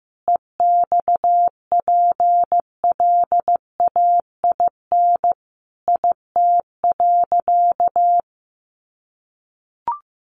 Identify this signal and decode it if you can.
Morse code, heard as a keyed tone, about 14 wpm, 700 Hz. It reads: EXPLAIN IT.
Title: Sentences from Top 1100 Words 15wpm